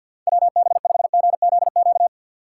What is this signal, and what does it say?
Morse code, heard as a keyed tone, about 53 wpm, 700 Hz. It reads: WB5GZQ
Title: USA Call Signs - 6 Characters - 3x Character Spacing - 50wpm